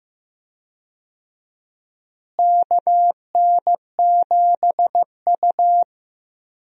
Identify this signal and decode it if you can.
Morse code, heard as a keyed tone, about 15 wpm, 700 Hz. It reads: KN7U
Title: USA Call Signs - 4 Characters - Repeated 3x - 15wpm